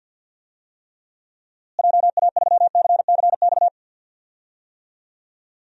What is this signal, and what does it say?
Morse code, heard as a keyed tone, about 48 wpm, 700 Hz. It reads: JA3CCX
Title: DXCC-100 Call Signs - 6 Characters - Repeated 3x - 50wpm